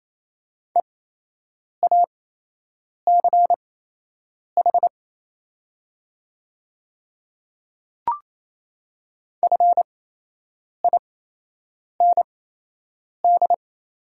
Morse code, heard as a keyed tone, about 28 wpm, spaced as for 8 wpm, 700 Hz. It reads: EACH FIND
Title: 8x Character Spacing - Top 100 Words - 28wpm